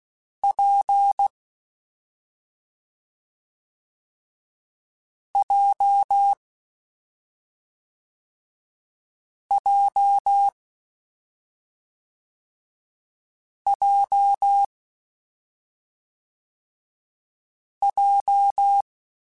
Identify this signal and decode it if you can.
Morse code, heard as a keyed tone, about 16 wpm, 800 Hz. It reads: P J J J J
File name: Difficult Combos / Difficult Combos-mp3s / WPJ WPJ1 @5